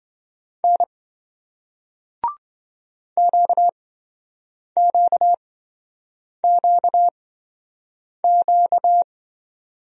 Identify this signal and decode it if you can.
Morse code, heard as a keyed tone, about 26 wpm, spaced as for 7 wpm, 700 Hz. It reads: N QQQQ